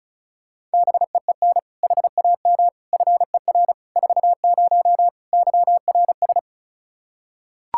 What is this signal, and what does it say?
Morse code, heard as a keyed tone, about 36 wpm, 700 Hz. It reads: BEEN HAM FER 40 YRS
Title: Practice QSO - Exchange 3 - More Info - 35wpm